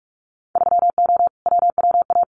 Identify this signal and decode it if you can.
Morse code, heard as a keyed tone, about 46 wpm, 700 Hz. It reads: 3O WWA
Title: Winter Field Day Exchanges Encore 45wpm